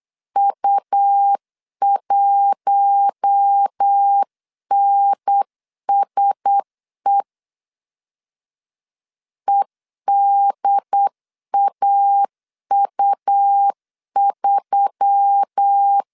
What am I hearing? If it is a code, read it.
Morse code, heard as a keyed tone, about 8 wpm, 800 Hz. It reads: U1NSE EDAU3
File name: CW / DET / 1 6 1500C S C 8wpm 5219